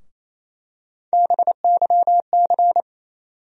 Morse code, heard as a keyed tone, about 28 wpm, 700 Hz. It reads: BYC
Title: USA Call Sign Postfixes - 3 Letters 28wpm